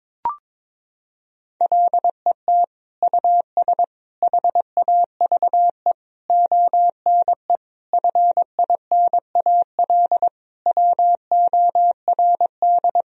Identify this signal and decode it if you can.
Morse code, heard as a keyed tone, about 22 wpm, 700 Hz. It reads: LET US HAVE ONE FINAL WORD
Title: Sentences from Top 1000 Words - Review Part 2 - 22wpm